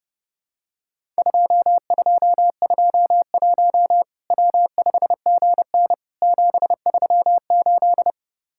Morse code, heard as a keyed tone, about 30 wpm, 700 Hz. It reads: 2221 W5GN 738
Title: Worked All Europe DX Contest - QTC Exchange - 30wpm